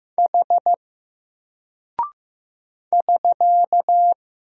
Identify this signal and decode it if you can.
Morse code, heard as a keyed tone, about 15 wpm, 700 Hz. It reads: H <SK>
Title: Single Character (Letter:Number:Punctuation:Prosign) Alt 15wpm